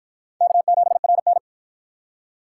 Morse code, heard as a keyed tone, about 50 wpm, 700 Hz. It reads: K7RN